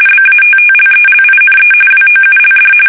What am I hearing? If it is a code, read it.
RTTY, about 45.45 baud. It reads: BATTLELINE WOULD
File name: fsk850 / battleshipnj